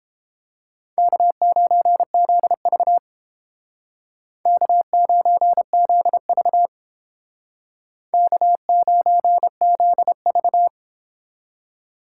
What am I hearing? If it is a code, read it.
Morse code, heard as a keyed tone, about 30 wpm, 700 Hz. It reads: K9ZV K9ZV K9ZV